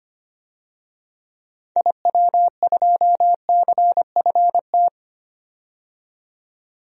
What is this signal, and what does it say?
Morse code, heard as a keyed tone, about 25 wpm, 700 Hz. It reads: IW2CFT
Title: DXCC-100 Call Signs - 6 Characters - 25wpm